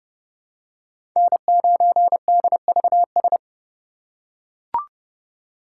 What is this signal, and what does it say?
Morse code, heard as a keyed tone, about 30 wpm, 700 Hz. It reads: N9DVS
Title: USA Call Signs - 5 Characters - 3x Character Spacing - 30wpm